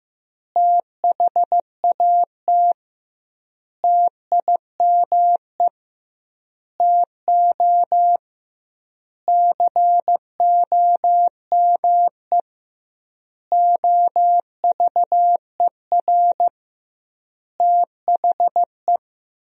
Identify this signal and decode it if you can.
Morse code, heard as a keyed tone, about 15 wpm, 700 Hz. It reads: THAT TIME TO COME OVER THE
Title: 2x Word-Spacing - Sentences from Top 300 Words - 15wpm